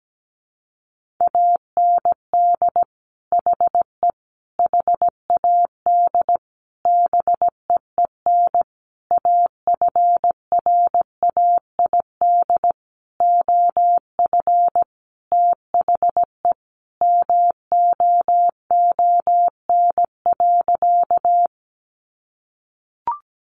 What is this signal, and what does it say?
Morse code, heard as a keyed tone, about 17 wpm, 700 Hz. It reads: AND HE HAD BEEN AFRAID OF THE MOON.
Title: Sentences from Top 1500 Words 17wpm